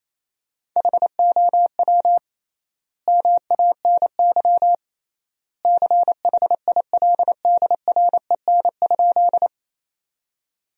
Morse code, heard as a keyed tone, about 28 wpm, 700 Hz. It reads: HOW MANY CHILDREN?
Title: 3x Word-Spacing - Sentences from Top 300 Words - 28wpm